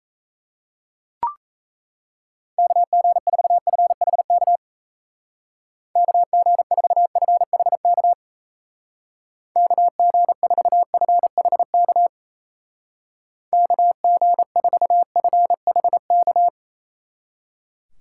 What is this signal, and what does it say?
Morse code, heard as a keyed tone, about 36 wpm, 700 Hz. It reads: KG4FHK KG4FHK KG4FHK KG4FHK